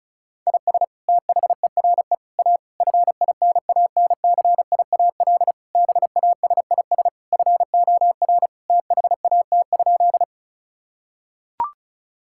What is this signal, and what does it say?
Morse code, heard as a keyed tone, about 36 wpm, 700 Hz. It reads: IS THERE A FINANCIAL BASIS FOR THAT?
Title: Sentences from Top 1000 Words - Review Part 5 - 35wpm